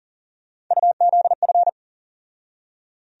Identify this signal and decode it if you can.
Morse code, heard as a keyed tone, about 40 wpm, 700 Hz. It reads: UZF
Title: USA Call Sign Postfixes - 3 Letters 40wpm